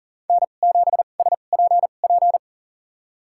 Morse code, heard as a keyed tone, about 40 wpm, spaced as for 29 wpm, 700 Hz. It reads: N7SPP
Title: USA Call Signs - 5 Characters - 2x Character Spacing - 40wpm